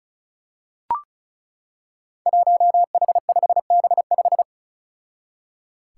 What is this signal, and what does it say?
Morse code, heard as a keyed tone, about 34 wpm, 700 Hz. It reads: 1H5B5
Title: Five Challenging Letter-Number Combinations 35wpm